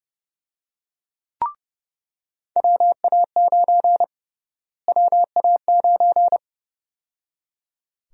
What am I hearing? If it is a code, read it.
Morse code, heard as a keyed tone, about 30 wpm, 700 Hz. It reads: WA9 WA9
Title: USA Call Sign Prefixes - Repeated 2x - 30wpm